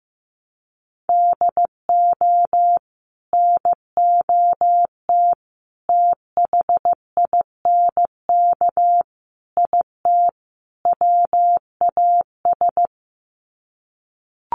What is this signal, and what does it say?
Morse code, heard as a keyed tone, about 15 wpm, 700 Hz. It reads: DO NOT THINK IT WAS
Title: Sets of 5 Words Encore 15wpm